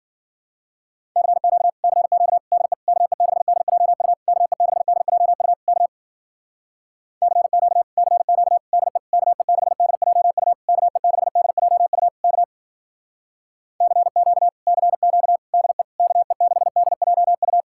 Morse code, heard as a keyed tone, about 56 wpm, 700 Hz. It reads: CQ CQ DE KE6DJU KE6DJU K CQ CQ DE KE6DJU KE6DJU K CQ CQ DE KE6DJU